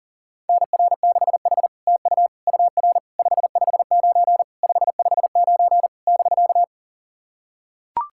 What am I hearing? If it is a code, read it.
Morse code, heard as a keyed tone, about 40 wpm, 700 Hz. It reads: NR6H TU UR 559 559 <BK>
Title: SOTA QSO Practice 40wpm